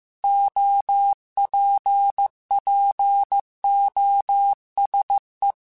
Morse code, heard as a keyed tone, about 15 wpm, 800 Hz. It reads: OPPOSE
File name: Technical Subjects / Technical Subjects-mp3s / XL @13